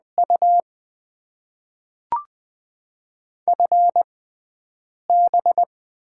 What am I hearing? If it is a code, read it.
Morse code, heard as a keyed tone, about 20 wpm, spaced as for 4 wpm, 700 Hz. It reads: UFB
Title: Top 20 QSO Elements - 6x Character Spacing - 20wpm